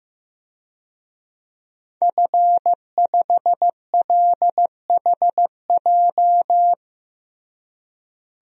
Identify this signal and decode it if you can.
Morse code, heard as a keyed tone, about 16 wpm, 700 Hz. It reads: F5LHJ
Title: DXCC-100 Call Signs - 5 Characters - Repeated 3x - 15wpm